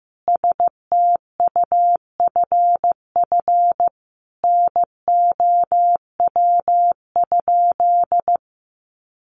Morse code, heard as a keyed tone, about 15 wpm, 700 Hz. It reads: STUFF NOW?
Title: Sentences from Top 1400 Words 15wpm